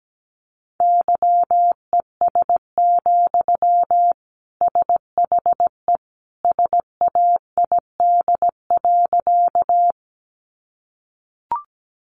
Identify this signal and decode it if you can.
Morse code, heard as a keyed tone, about 17 wpm, 700 Hz. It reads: YES, SHE SAID.
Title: Sentences from Top 500 Words 17wpm